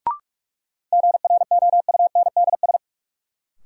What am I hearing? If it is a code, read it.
Morse code, heard as a keyed tone, about 44 wpm, 700 Hz. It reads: GROUNDS
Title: Sentences from Top 1300 Words - New Word Drill - 45wpm